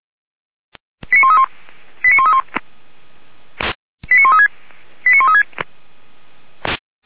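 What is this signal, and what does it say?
Five-tone selective call (ZVEI), heard as quick runs of five tones, tones of about 70 ms. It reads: 89131 89131 89136 89136
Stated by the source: Five-tone ZVEI selective call received by SDR, from the audio server of the Feuerwehr Orlamünde fire brigade.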